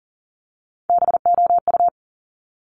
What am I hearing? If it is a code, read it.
Morse code, heard as a keyed tone, about 40 wpm, 700 Hz. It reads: BOU
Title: Top Three Letter Combinations in Words 40wpm